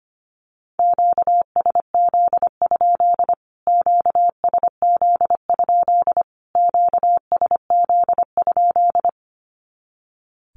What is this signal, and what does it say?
Morse code, heard as a keyed tone, about 25 wpm, 700 Hz. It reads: QSZ? QSZ? QSZ?
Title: Q-Codes Repeated 3x 25wpm